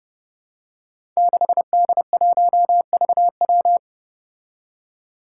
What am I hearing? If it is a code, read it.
Morse code, heard as a keyed tone, about 30 wpm, 700 Hz. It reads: 6D1VW
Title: Five Challenging Letter-Number Combinations 30wpm